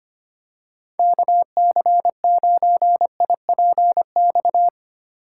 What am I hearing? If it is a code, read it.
Morse code, heard as a keyed tone, about 25 wpm, 700 Hz. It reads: KC9IPX
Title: USA Call Signs - 6 Characters - 8x Character Spacing - 25wpm